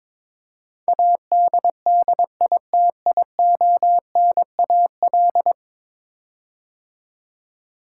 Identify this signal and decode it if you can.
Morse code, heard as a keyed tone, about 22 wpm, 700 Hz. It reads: ADDITIONAL
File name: practice / 22wpm / 10 Letter Words - First 3 Letters Given - 22wpm